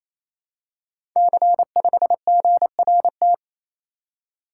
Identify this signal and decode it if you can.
Morse code, heard as a keyed tone, about 29 wpm, 700 Hz. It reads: C5GRT
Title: Code Groups - 5 Letter:Numbers - 28wpm